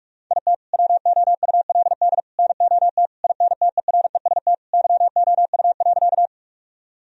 Morse code, heard as a keyed tone, about 46 wpm, 700 Hz. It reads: IT WOULD NOT INTEREST YOU.
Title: Sentences from Top 1000 Words - Review Part 2 - 45wpm